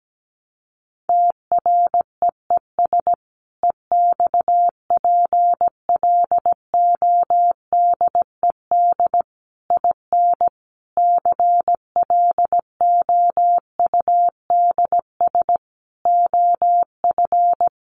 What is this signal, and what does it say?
Morse code, heard as a keyed tone, about 17 wpm, 700 Hz. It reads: TREES EXPLODED IN CLOUDS OF
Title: A Sound Of Thunder 17wpm